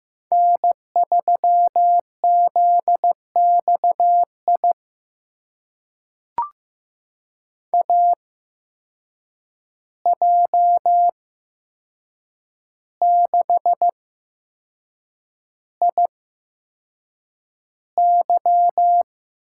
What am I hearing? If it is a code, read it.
Morse code, heard as a keyed tone, about 15 wpm, spaced as for 6 wpm, 700 Hz. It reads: N3ZXI AJ6IY
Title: USA Call Signs - 5 Characters - 8x Character Spacing - 15wpm